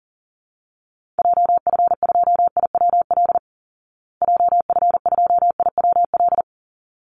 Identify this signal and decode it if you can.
Morse code, heard as a keyed tone, about 37 wpm, 700 Hz. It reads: JF2IWL JF2IWL
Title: Call Signs Repeated 3x - 40wpm